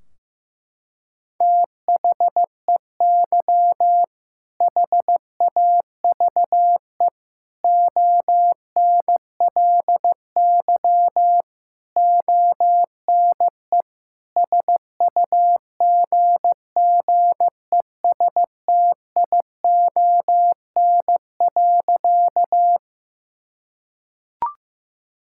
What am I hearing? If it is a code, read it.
Morse code, heard as a keyed tone, about 15 wpm, 700 Hz. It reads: THEY HAVE ONLY ONE SUGGESTION.
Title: Sentences from Top 1600 Words 15wpm